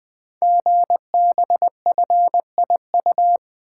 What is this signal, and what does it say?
Morse code, heard as a keyed tone, about 20 wpm, 700 Hz. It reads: GBFIU
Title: Code Groups - 5 Letters - 20wpm